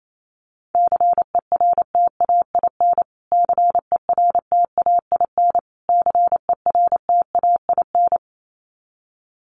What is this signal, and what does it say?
Morse code, heard as a keyed tone, about 28 wpm, 700 Hz. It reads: CERTAIN CERTAIN CERTAIN
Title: Top 500 Words Repeated 28wpm